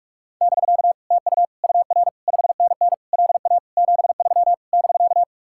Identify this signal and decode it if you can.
Morse code, heard as a keyed tone, about 49 wpm, 700 Hz. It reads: <BK> TU UR 5NN LA 73 <BK>